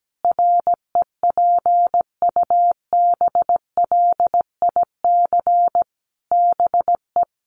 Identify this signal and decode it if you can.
Morse code, heard as a keyed tone, about 17 wpm, 700 Hz. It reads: REPUBLIC BE
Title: Sentences from Top 1500 Words 17wpm